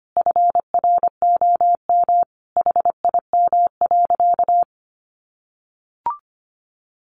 Morse code, heard as a keyed tone, about 25 wpm, 700 Hz. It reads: FROM HIM.
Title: Sentences from Top 1200 Words 25wpm